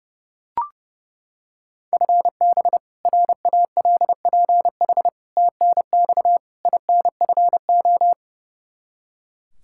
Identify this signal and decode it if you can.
Morse code, heard as a keyed tone, about 30 wpm, 700 Hz. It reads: FB RALPH TNX INFO
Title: Practice QSO - Exchange 3 - More Info - 30wpm